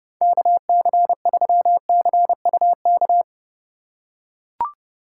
Morse code, heard as a keyed tone, about 31 wpm, 700 Hz. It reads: KC3CUK